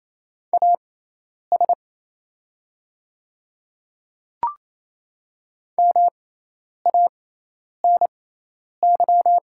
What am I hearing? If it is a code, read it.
Morse code, heard as a keyed tone, about 28 wpm, spaced as for 10 wpm, 700 Hz. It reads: AS MANY
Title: 6x Character Spacing - Top 500 Words - 28wpm